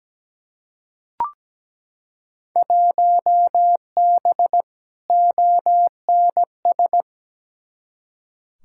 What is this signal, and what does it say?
Morse code, heard as a keyed tone, about 17 wpm, 700 Hz. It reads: 1B ONS
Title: Field Day Exchanges Encore - 17wpm